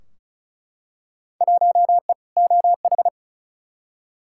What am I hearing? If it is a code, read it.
Morse code, heard as a keyed tone, about 36 wpm, 700 Hz. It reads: 1E OH
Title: Field Day Exchanges - 35wpm